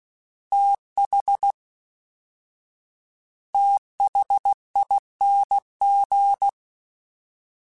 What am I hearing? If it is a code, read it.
Morse code, heard as a keyed tone, about 16 wpm, 800 Hz. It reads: TH THING
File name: TH / TH mp3s / TH Prefixes 1x1 @13